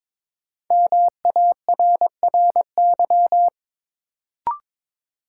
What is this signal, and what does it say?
Morse code, heard as a keyed tone, about 22 wpm, 700 Hz. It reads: MARRY
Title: Sentences from Top 1200 Words - New Word Drill - 22wpm